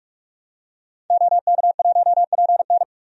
Morse code, heard as a keyed tone, about 42 wpm, 700 Hz. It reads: OK1PN
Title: DXCC-100 Call Signs - 5 Characters - 45wpm